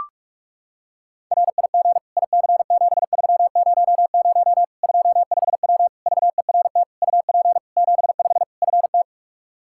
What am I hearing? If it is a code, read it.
Morse code, heard as a keyed tone, about 45 wpm, 700 Hz. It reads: RIG IC7300 25W VERT UP 75 FT
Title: Practice QSO - Exchange 2 - Info - 45wpm